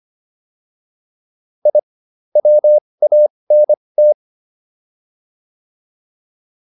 Morse code, heard as a keyed tone, about 25 wpm, spaced as for 20 wpm, 600 Hz. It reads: I WANT